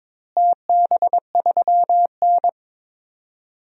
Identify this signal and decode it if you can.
Morse code, heard as a keyed tone, about 22 wpm, 700 Hz. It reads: TB3N
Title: DXCC-100 Call Signs - 4 Characters - Repeated 2x - 22wpm